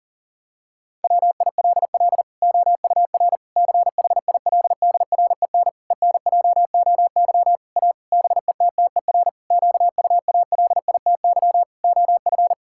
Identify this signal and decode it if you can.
Morse code, heard as a keyed tone, about 40 wpm, 700 Hz. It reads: WILL OUR CHILDREN ENJOY A BETTER QUALITY OF
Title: Sentences from Top 1000 Words - Review Part 5 - 40wpm